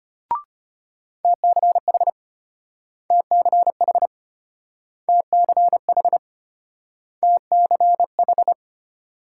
Morse code, heard as a keyed tone, about 32 wpm, 700 Hz. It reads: TCH TCH TCH TCH